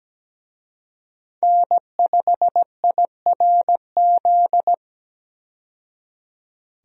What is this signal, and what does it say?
Morse code, heard as a keyed tone, about 17 wpm, 700 Hz. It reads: N5IRZ
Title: USA Call Signs - 5 Characters - 17wpm